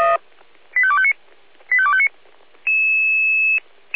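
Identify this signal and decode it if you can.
Five-tone selective call (ZVEI), heard as quick runs of five tones, tones of about 70 ms. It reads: 85259 85259
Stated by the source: ZVEI five-tone selective call received on an SDR.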